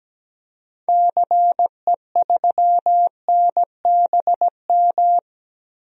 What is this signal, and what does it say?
Morse code, heard as a keyed tone, about 17 wpm, 700 Hz. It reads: CE3NBM
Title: DXCC-100 Call Signs - 6 Characters - Repeated 3x - 17wpm